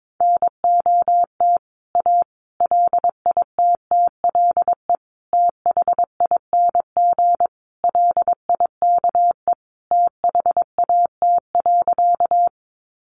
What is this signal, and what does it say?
Morse code, heard as a keyed tone, about 22 wpm, 700 Hz. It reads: NOT A LITTLE THING LIKE THAT.
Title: A Sound Of Thunder 22wpm